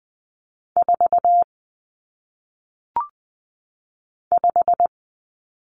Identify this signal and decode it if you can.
Morse code, heard as a keyed tone, about 20 wpm, 700 Hz. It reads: 4 5
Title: Numbers - 1-Digit 20wpm